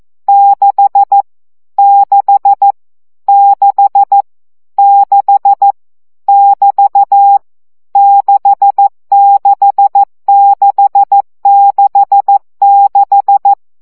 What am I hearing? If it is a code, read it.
Morse code, heard as a keyed tone, about 14 wpm, 800 Hz. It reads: 6 6 6 6 <BT> 66666
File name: CW gyakorlas 6 12 baud